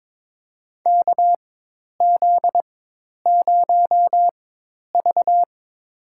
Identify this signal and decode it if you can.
Morse code, heard as a keyed tone, about 22 wpm, spaced as for 10 wpm, 700 Hz. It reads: KZ0V
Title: USA Call Signs - 4 Characters - 4x Character Spacing - 22wpm